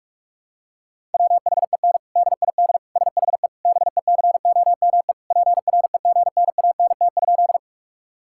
Morse code, heard as a keyed tone, about 46 wpm, 690 Hz. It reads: WHEN DID SHE BECOME PREGNANT?
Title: Sentences from Top 1000 Words - Review Part 6 - 45wpm